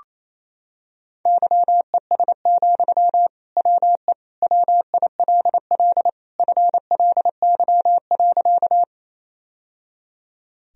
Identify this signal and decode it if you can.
Morse code, heard as a keyed tone, about 28 wpm, 700 Hz. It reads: YES, WE WILL FLY.
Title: Sentences from Top 1000 Words - Review Part 2 - 28wpm